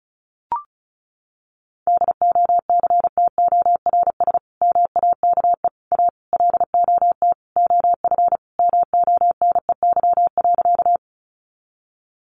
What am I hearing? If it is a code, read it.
Morse code, heard as a keyed tone, about 34 wpm, 700 Hz. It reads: DOCTORS MAKE A LOT OF MONEY.
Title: Easy Phrases 35wpm